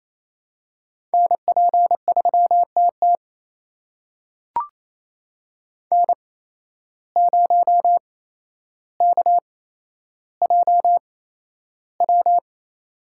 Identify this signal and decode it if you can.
Morse code, heard as a keyed tone, about 28 wpm, spaced as for 12 wpm, 700 Hz. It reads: NP3TT N0KJW
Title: USA Call Signs - 8x Character Spacing - 28wpm